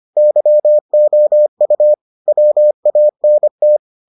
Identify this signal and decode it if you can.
Morse code, heard as a keyed tone, about 25 wpm, 600 Hz. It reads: YOU WANT